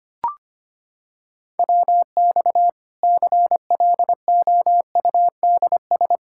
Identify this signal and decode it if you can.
Morse code, heard as a keyed tone, about 25 wpm, 700 Hz. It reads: WX CLOUDS
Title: Practice QSO - Exchange 2 - Info - 25wpm